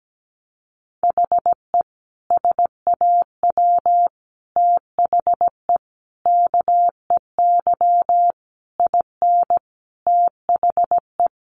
Morse code, heard as a keyed tone, about 17 wpm, 700 Hz. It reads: HE SAW THE KEY IN THE
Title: Sentences from Top 1300 Words 17wpm